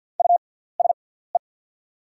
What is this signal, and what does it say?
Morse code, heard as a keyed tone, about 48 wpm, spaced as for 17 wpm, 690 Hz. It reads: USE
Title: 6x Character Spacing - Top 100 Words - 50wpm